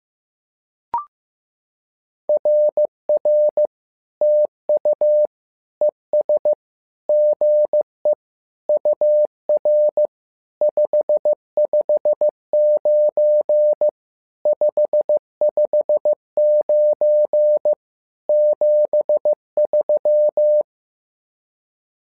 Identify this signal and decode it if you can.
Morse code, heard as a keyed tone, about 15 wpm, 600 Hz. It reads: RR TU ES GE UR 559 559 73
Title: SOTA QSO Practice 15wpm